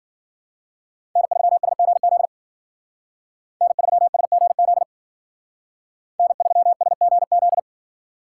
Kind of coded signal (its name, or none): Morse code